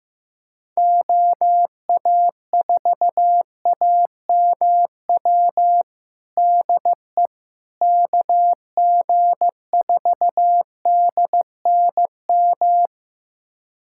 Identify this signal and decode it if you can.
Morse code, heard as a keyed tone, about 15 wpm, 700 Hz. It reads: OA4AMW DE KG4DNM
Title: USA Callsign DE Callsign 15wpm